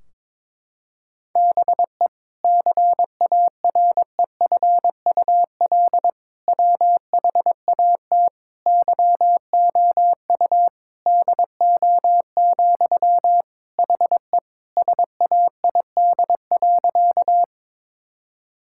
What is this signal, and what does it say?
Morse code, heard as a keyed tone, about 22 wpm, 700 Hz. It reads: BE CAREFUL WHAT YOU DO, HE SAID.
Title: Sentences from Top 1600 Words 22wpm